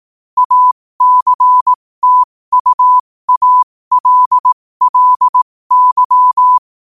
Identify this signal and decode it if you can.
Morse code, heard as a keyed tone, about 18 wpm, spaced as for 15 wpm, 1000 Hz. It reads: ACTUALLY
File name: qotdfiles / 15wpm